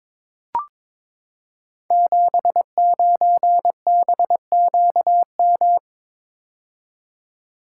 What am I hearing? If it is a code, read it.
Morse code, heard as a keyed tone, about 22 wpm, 700 Hz. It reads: 79BQM
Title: Code Groups - 5 Letter:Numbers - 22wpm